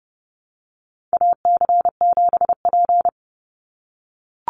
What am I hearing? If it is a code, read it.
Morse code, heard as a keyed tone, about 30 wpm, 700 Hz. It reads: AC7P
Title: NA SOTA Chaser Callsigns 30wpm